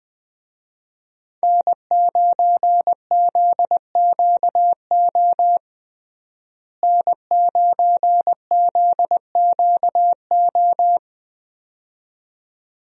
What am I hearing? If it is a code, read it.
Morse code, heard as a keyed tone, about 20 wpm, 700 Hz. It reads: N9ZQO N9ZQO
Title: USA Call Signs - 5 Characters - Repeated 2x - 20wpm